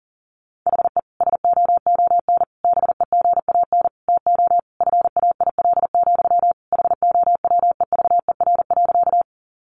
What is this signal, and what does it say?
Morse code, heard as a keyed tone, about 40 wpm, 700 Hz. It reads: HE SOON BEGAN TO FAIL, HOWEVER.
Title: Sentences from Top 1100 Words 40wpm